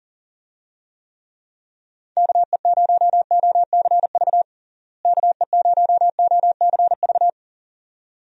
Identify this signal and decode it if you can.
Morse code, heard as a keyed tone, about 37 wpm, 700 Hz. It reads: KE0OCV KE0OCV
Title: USA Call Signs - 6 Characters - Repeated 2x - 40wpm